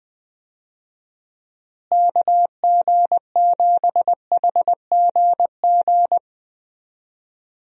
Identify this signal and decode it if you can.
Morse code, heard as a keyed tone, about 20 wpm, 700 Hz. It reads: KG7HGG
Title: USA Call Signs - 6 Characters - Repeated 3x - 20wpm